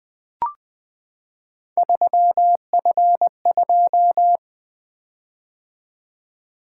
Morse code, heard as a keyed tone, about 20 wpm, 700 Hz. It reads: 3F2
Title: Three Letter-Number Combinations - 20wpm